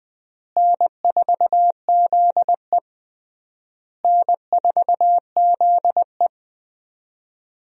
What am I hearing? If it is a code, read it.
Morse code, heard as a keyed tone, about 21 wpm, 700 Hz. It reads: N4ZE N4ZE
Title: USA Call Signs - 4 Characters - Repeated 3x - 20wpm